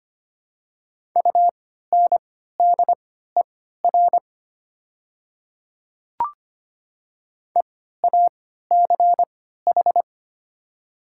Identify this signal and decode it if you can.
Morse code, heard as a keyed tone, about 25 wpm, spaced as for 14 wpm, 700 Hz. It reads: UNDER EACH
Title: 3x Character Spacing - Top 200 Words - 25wpm